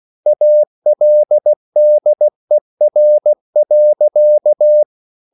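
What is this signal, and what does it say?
Morse code, heard as a keyed tone, about 16 wpm, 600 Hz. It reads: ALDER.
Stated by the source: Slow (QRS) Morse code at 16 wpm, from CW med Gnister.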